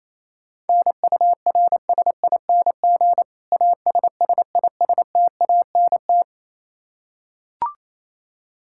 Morse code, heard as a keyed tone, about 28 wpm, 700 Hz. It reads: NURSING ASSISTANT
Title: Practice QSO - Exchange 3 - More Info - 28wpm